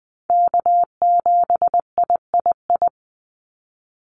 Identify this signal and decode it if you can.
Morse code, heard as a keyed tone, about 20 wpm, 700 Hz. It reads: K7III
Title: NA SOTA Chaser Callsigns 20wpm